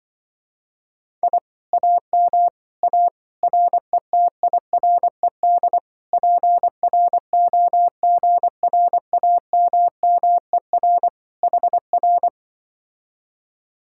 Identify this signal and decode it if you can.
Morse code, heard as a keyed tone, about 24 wpm, 700 Hz. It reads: I AM A RETIRED PROGRAMMER HR
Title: Speed-Racing Common QSO Phrases Encore - 20wpm